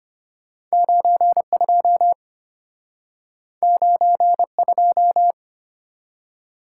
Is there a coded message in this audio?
Yes: Morse code from 0.7 to 5.3 s, heard as a keyed tone, about 34 wpm, 700 Hz. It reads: 92 92